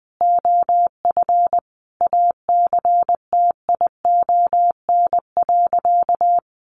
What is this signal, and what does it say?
Morse code, heard as a keyed tone, about 19 wpm, 700 Hz. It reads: OF ACTION.
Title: Sentences from Top 700 Words 20wpm